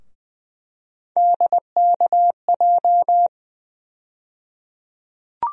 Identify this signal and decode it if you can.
Morse code, heard as a keyed tone, about 21 wpm, 700 Hz. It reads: DKJ
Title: USA Call Sign Postfixes - 3 Letters 20wpm